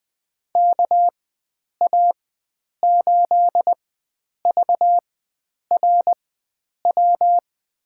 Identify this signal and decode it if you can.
Morse code, heard as a keyed tone, about 20 wpm, spaced as for 9 wpm, 700 Hz. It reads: KA8VRW
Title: USA Call Signs - 6 Characters - 4x Character Spacing - 20wpm